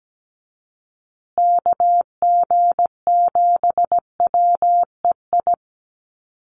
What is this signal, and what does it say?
Morse code, heard as a keyed tone, about 17 wpm, 700 Hz. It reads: KG7WEI
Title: USA Call Signs Repeated 2x 17wpm